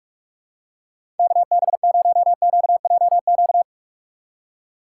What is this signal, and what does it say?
Morse code, heard as a keyed tone, about 46 wpm, 700 Hz. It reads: KB0QJQ
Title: USA Call Signs - 6 Characters - 4x Character Spacing - 45wpm